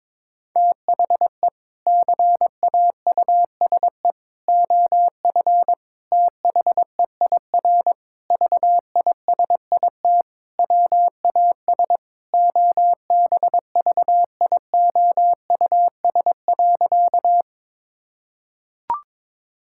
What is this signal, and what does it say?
Morse code, heard as a keyed tone, about 22 wpm, 700 Hz. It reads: THE CAUSE OF THEIR VISIT WAS OBVIOUS.
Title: Sentences from Top 1600 Words 22wpm